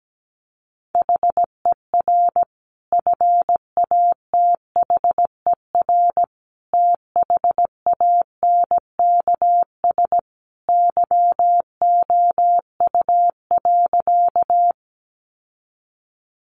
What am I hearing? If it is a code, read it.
Morse code, heard as a keyed tone, about 17 wpm, 700 Hz. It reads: HER FATHER THANKS YOU.
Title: Sentences from Top 600 Words 17wpm